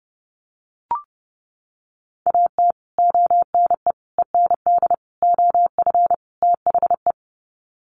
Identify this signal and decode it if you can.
Morse code, heard as a keyed tone, about 28 wpm, 700 Hz. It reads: AT ONE END OF THE
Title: Sets of 5 Words Encore 30wpm